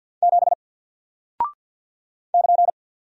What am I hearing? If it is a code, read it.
Morse code, heard as a keyed tone, about 48 wpm, 700 Hz. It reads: 7 <KN>
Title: Single Character (Letter:Number:Punctuation:Prosign) Alt 50wpm